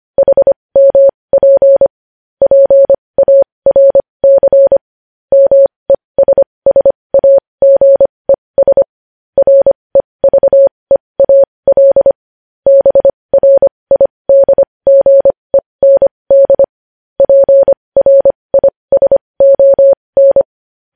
Morse code, heard as a keyed tone, about 25 wpm, spaced as for 20 wpm, 550 Hz. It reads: HMP PARC MESSAGES REVEAL BRIDGEND PRISON
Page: News Headlines in Morse Code at 20 WPM